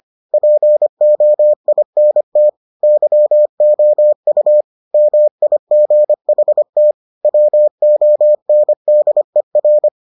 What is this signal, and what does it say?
Morse code, heard as a keyed tone, about 25 wpm, 600 Hz. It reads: POINT YOU MIGHT WONDER